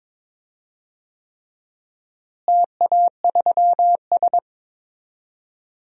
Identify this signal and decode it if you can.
Morse code, heard as a keyed tone, about 22 wpm, 700 Hz. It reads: TA3S
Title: DXCC-100 Call Signs - 4 Characters - Repeated 3x - 22wpm